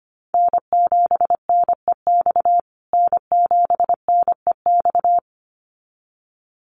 Morse code, heard as a keyed tone, about 25 wpm, 700 Hz. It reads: N7NEX N7NEX
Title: USA Call Signs Repeated 2x 25wpm